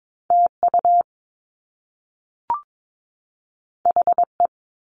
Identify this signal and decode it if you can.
Morse code, heard as a keyed tone, about 22 wpm, 700 Hz. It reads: TU HE
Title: Top Two Letter Combinations in Words 22wpm